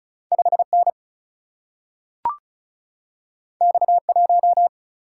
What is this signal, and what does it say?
Morse code, heard as a keyed tone, about 36 wpm, 700 Hz. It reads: 5 N X 1
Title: Single-Letter Single-Number 35wpm